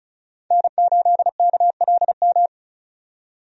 Morse code, heard as a keyed tone, about 36 wpm, 700 Hz. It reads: N8KLM
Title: USA Call Signs - 5 Characters - 35wpm